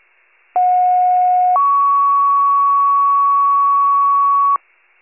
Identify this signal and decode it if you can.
Two-tone page, heard as a one-second tone, then a three-second tone, about 730 then 1120 Hz. It is heard from 0.5 to 4.6 s.